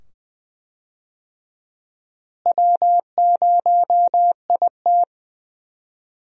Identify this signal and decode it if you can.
Morse code, heard as a keyed tone, about 20 wpm, 700 Hz. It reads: W0IT
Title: USA Call Signs - 4 Characters - Repeated 2x - 20wpm